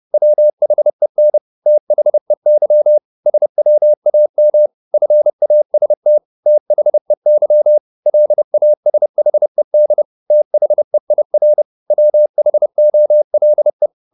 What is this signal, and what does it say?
Morse code, heard as a keyed tone, about 30 wpm, 600 Hz. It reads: WHEN THEY SWAM FAST THEY LASHED THEIR WHOLE